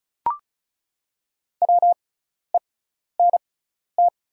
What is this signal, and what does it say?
Morse code, heard as a keyed tone, about 36 wpm, spaced as for 12 wpm, 700 Hz. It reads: WENT
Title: 6x Character Spacing - Top 200 Words - 35wpm